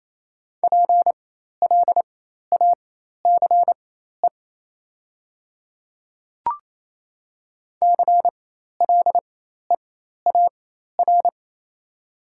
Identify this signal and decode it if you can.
Morse code, heard as a keyed tone, about 28 wpm, spaced as for 13 wpm, 690 Hz. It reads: PLACE CLEAR
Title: 4x Character Spacing - Top 500 Words - 28wpm